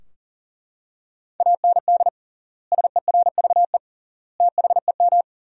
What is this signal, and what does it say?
Morse code, heard as a keyed tone, about 37 wpm, 700 Hz. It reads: AND SERVE THEM
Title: 3x Word-Spacing - Sentences from Top 400 Words - 40wpm